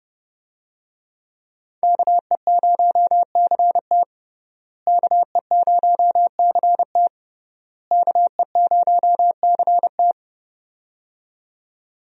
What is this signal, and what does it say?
Morse code, heard as a keyed tone, about 30 wpm, 700 Hz. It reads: KE0CT KE0CT KE0CT
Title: USA Call Signs - 5 Characters - Repeated 3x - 30wpm